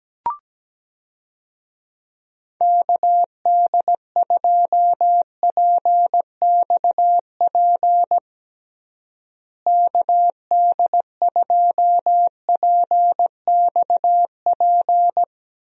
Morse code, heard as a keyed tone, about 17 wpm, 700 Hz. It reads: KD2PXP KD2PXP
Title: USA Call Signs - 6 Characters - Repeated 3x - 17wpm